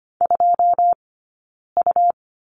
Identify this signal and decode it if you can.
Morse code, heard as a keyed tone, about 25 wpm, 700 Hz. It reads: 2 U
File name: Lesson 17d - Letter u - Two Characters 25wpm